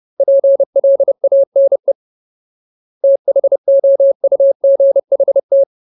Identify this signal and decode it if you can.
Morse code, heard as a keyed tone, about 30 wpm, 550 Hz. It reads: PLANE THOUGHT
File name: Ultimate Word List / 30WPM Ultimate